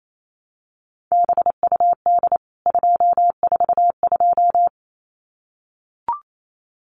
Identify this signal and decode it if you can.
Morse code, heard as a keyed tone, about 28 wpm, 700 Hz. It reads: BUD 242
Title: CWT Exchange 28wpm